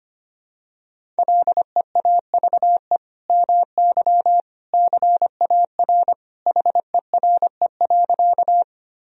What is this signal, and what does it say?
Morse code, heard as a keyed tone, about 25 wpm, 700 Hz. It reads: LEAVE MY CAR HERE.
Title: Sentences from Top 1000 Words - Review Part 1 - 25wpm